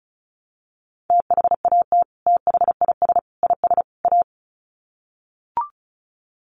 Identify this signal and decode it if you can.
Morse code, heard as a keyed tone, about 34 wpm, 700 Hz. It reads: THAT THIS IS A
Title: Sets of 4 Words 35wpm